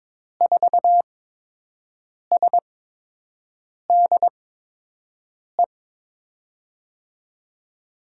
Morse code, heard as a keyed tone, about 23 wpm, spaced as for 6 wpm, 700 Hz. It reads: 4SDE
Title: USA Call Signs - 8x Character Spacing - 22wpm